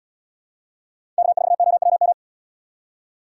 Morse code, heard as a keyed tone, about 77 wpm, 700 Hz. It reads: K4OWM